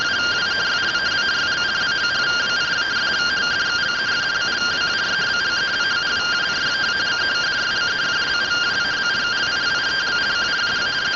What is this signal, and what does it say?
Two-tone teleprinter signal, about 170 Hz shift.